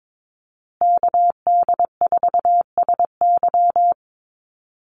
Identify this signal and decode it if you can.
Morse code, heard as a keyed tone, about 22 wpm, 700 Hz. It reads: KD4SY
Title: USA Call Signs Repeated 3x 22wpm